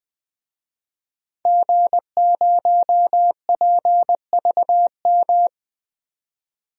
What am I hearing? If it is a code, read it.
Morse code, heard as a keyed tone, about 20 wpm, 700 Hz. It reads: G0PVM
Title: DXCC-100 Call Signs - 5 Characters - 20wpm